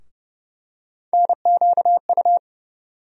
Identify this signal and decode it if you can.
Morse code, heard as a keyed tone, about 30 wpm, 700 Hz. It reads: NQU
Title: USA Call Sign Postfixes - 3 Letters 30wpm